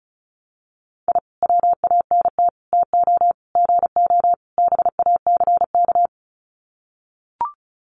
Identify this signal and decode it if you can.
Morse code, heard as a keyed tone, about 34 wpm, 700 Hz. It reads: I WANT TO GO BACK
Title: Sets of 5 Words Encore 35wpm